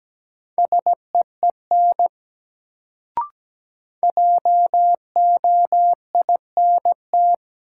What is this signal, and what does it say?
Morse code, heard as a keyed tone, about 17 wpm, 700 Hz. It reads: SEEN JOINT
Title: Sets of 5 Words Encore - New Word Drill - 17wpm